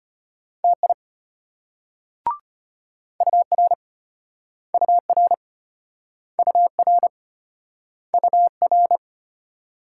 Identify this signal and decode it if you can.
Morse code, heard as a keyed tone, about 35 wpm, 700 Hz. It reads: TI UR UR UR UR